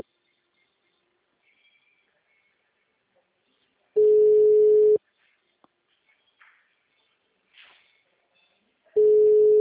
Ringback tone (Europe, 425 Hz).